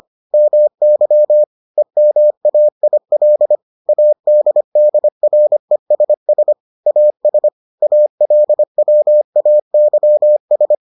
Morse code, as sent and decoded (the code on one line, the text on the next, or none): -- -.-- / . -- .- .. .-.. / .- -.. -.. .-. ... ... / .- ... / .- .-.. .-- .- -.-- ...
MY EMAIL ADDRESS AS ALWAYS